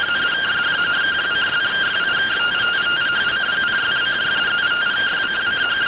Two-tone teleprinter signal, about 170 Hz shift.